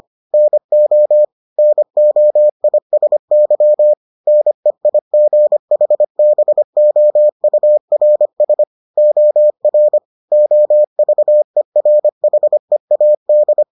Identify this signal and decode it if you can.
Morse code, heard as a keyed tone, about 25 wpm, 600 Hz. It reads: NO NOISY NEIGHBOURS OR OVERHEAD